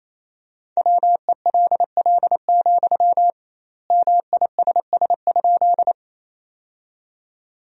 Morse code, heard as a keyed tone, about 28 wpm, 700 Hz. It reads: WELL, MISS?
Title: 2x Word-Spacing - Sentences from Top 500 Words - 28wpm